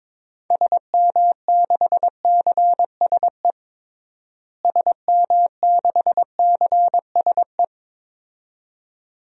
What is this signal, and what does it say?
Morse code, heard as a keyed tone, about 22 wpm, 700 Hz. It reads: SM6CSE SM6CSE
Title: DXCC-100 Call Signs - 6 Characters - Repeated 2x - 22wpm